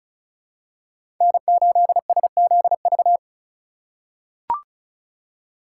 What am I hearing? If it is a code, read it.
Morse code, heard as a keyed tone, about 34 wpm, 700 Hz. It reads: N8SZV
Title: USA Call Signs - 5 Characters - 3x Character Spacing - 35wpm